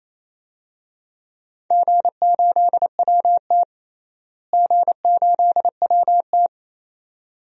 Morse code, heard as a keyed tone, about 28 wpm, 700 Hz. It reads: G8WT G8WT
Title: DXCC-100 Call Signs - 4 Characters - Repeated 2x - 28wpm